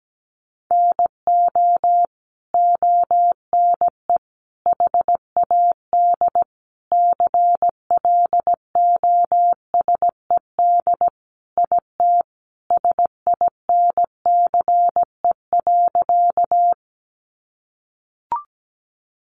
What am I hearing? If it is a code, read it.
Morse code, heard as a keyed tone, about 17 wpm, 700 Hz. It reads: NO ONE HAD CLOSED IT SINCE.
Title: Sentences from Top 1400 Words 17wpm